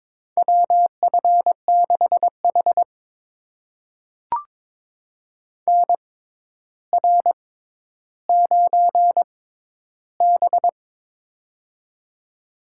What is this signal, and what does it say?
Morse code, heard as a keyed tone, about 22 wpm, spaced as for 11 wpm, 700 Hz. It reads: WF6H NR9B
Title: USA Call Signs - 4 Characters - 6x Character Spacing - 22wpm